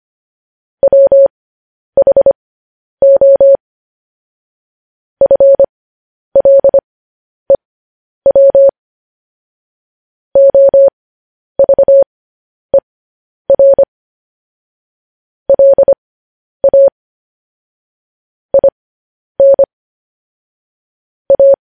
Morse code, heard as a keyed tone, about 25 wpm, spaced as for 10 wpm, 550 Hz. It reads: WHO FLEW OVER LA IN A